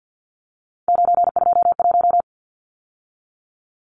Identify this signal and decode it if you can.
Morse code, heard as a keyed tone, about 50 wpm, 700 Hz. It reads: 921
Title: Numbers - 3-Digit 50wpm